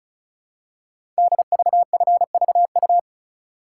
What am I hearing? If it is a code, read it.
Morse code, heard as a keyed tone, about 36 wpm, 700 Hz. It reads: DVFVU
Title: Code Groups - 5 Letters - 35wpm